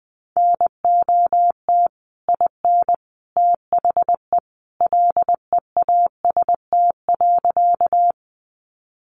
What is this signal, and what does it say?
Morse code, heard as a keyed tone, about 20 wpm, 700 Hz. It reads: NOT IN THE LEAST.
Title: Sentences from Top 700 Words 20wpm